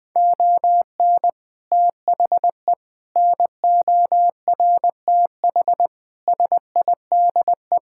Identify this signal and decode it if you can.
Morse code, heard as a keyed tone, about 21 wpm, 700 Hz. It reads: ON THE NORTH SIDE
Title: Sets of 6 Words - 20wpm